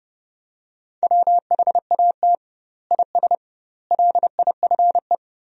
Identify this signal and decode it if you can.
Morse code, heard as a keyed tone, about 32 wpm, 700 Hz. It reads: WHAT IS LIFE
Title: 2x Word-Spacing - Sentences from Top 300 Words - 30wpm